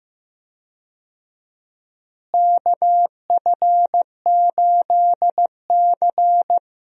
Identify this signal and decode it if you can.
Morse code, heard as a keyed tone, about 15 wpm, 700 Hz. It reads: KF8C
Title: USA Call Signs - 4 Characters - Repeated 3x - 15wpm